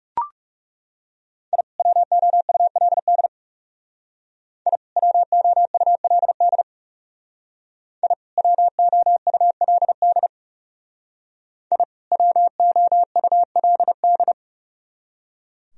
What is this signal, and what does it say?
Morse code, heard as a keyed tone, about 37 wpm, 700 Hz. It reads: I WOULD I WOULD I WOULD I WOULD
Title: Speed-Racing - Sets of 2 Words 30wpm